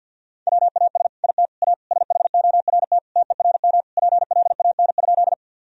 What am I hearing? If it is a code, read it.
Morse code, heard as a keyed tone, about 50 wpm, 700 Hz. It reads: WAS IT A SHORT TERM PLAN?
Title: Sentences from Top 1000 Words - Review Part 4 - 50wpm